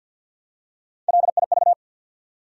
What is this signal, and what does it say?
Morse code, heard as a keyed tone, about 51 wpm, 700 Hz. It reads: LIV
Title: USA Common Names - Max 4 Letters - 50wpm